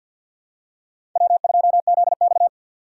Morse code, heard as a keyed tone, about 52 wpm, 700 Hz. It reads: W2ZX
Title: USA Call Signs - 4x Character Spacing - 50wpm